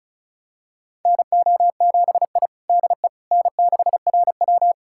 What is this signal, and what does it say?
Morse code, heard as a keyed tone, about 36 wpm, 700 Hz. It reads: NO7I DE N6RW
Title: USA Callsign DE Callsign - 4 Characters - 35wpm